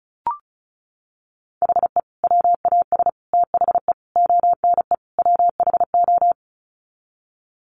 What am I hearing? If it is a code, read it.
Morse code, heard as a keyed tone, about 34 wpm, 700 Hz. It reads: HE WAS THE ONE WHO
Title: Sets of 5 Words Encore2 35wpm